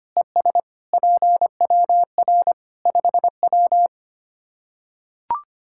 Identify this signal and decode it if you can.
Morse code, heard as a keyed tone, about 25 wpm, 700 Hz. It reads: ES PWR 5W
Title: Practice QSO - Exchange 2 - Info - 25wpm